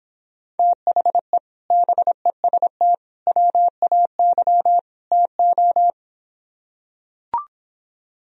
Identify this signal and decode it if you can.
Morse code, heard as a keyed tone, about 26 wpm, 700 Hz. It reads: THE BEST WAY TO